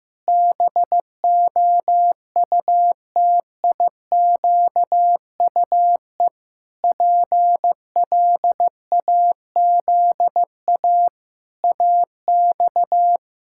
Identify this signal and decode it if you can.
Morse code, heard as a keyed tone, about 16 wpm, 700 Hz. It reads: BOUTIQUE PLAZA AX
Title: Pangram Challenge v2 - 15wpm